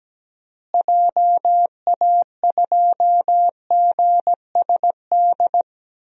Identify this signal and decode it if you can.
Morse code, heard as a keyed tone, about 17 wpm, 700 Hz. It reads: JA2GSD
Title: DXCC-100 Call Signs - 6 Characters - Repeated 2x - 17wpm